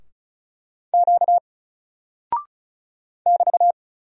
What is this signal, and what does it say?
Morse code, heard as a keyed tone, about 36 wpm, 700 Hz. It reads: Q <BT>
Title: Single Character (Letter:Number:Punctuation:Prosign) 35wpm